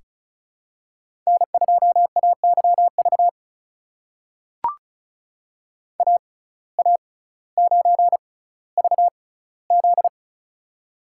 Morse code, heard as a keyed tone, about 33 wpm, spaced as for 18 wpm, 700 Hz. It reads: N2AYV AA9VZ
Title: USA Call Signs - 5 Characters - 6x Character Spacing - 35wpm